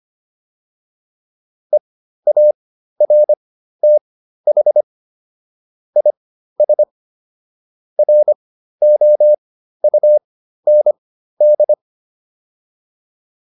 Morse code, heard as a keyed tone, about 25 wpm, spaced as for 13 wpm, 600 Hz. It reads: EARTH IS ROUND